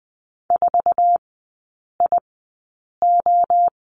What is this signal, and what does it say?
Morse code, heard as a keyed tone, about 20 wpm, spaced as for 8 wpm, 700 Hz. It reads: 4IO
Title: 047 Lesson 9e - Number 4 - Three Characters 20wpm